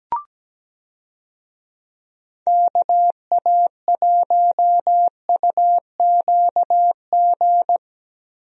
Morse code, heard as a keyed tone, about 17 wpm, 700 Hz. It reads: KA1UQG
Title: USA Call Signs - 6 Characters - Repeated 3x - 17wpm